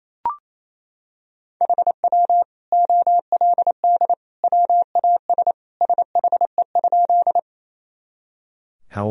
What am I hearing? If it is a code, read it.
Morse code, heard as a keyed tone, about 28 wpm, 700 Hz. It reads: HW OLD WAS SHE?
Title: Sentences from Top 200 Words and Abbreviations 28wpm